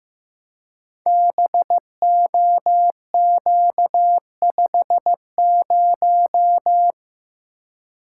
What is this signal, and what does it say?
Morse code, heard as a keyed tone, about 15 wpm, 700 Hz. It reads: BOQ50
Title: Code Groups - 5 Letter:Numbers - 15wpm